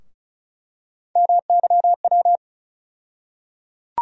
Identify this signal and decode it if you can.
Morse code, heard as a keyed tone, about 35 wpm, 700 Hz. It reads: MYW
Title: USA Call Sign Postfixes - 3 Letters 35wpm